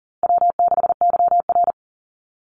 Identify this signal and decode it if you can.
Morse code, heard as a keyed tone, about 40 wpm, 700 Hz. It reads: W6YR
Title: Easy Call Signs Repeated 3x - 40wpm